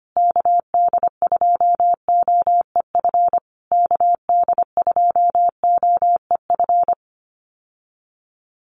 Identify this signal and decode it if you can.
Morse code, heard as a keyed tone, about 25 wpm, 700 Hz. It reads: KD2OEF KD2OEF
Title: USA Call Signs Repeated 3x 25wpm